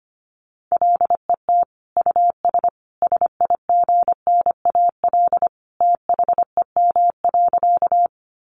Morse code, heard as a keyed tone, about 25 wpm, 700 Hz. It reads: LET US SIGNAL THEM.
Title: Sentences from Top 1400 Words 25wpm